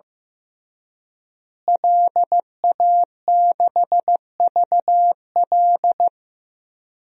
Morse code, heard as a keyed tone, about 15 wpm, 700 Hz. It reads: LA6VL
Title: DXCC-100 Call Signs - 5 Characters - Repeated 3x - 15wpm